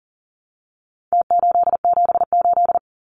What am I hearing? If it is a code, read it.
Morse code, heard as a keyed tone, about 40 wpm, 700 Hz. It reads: T878
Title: Numbers - Contest Style - 4-Digit 40wpm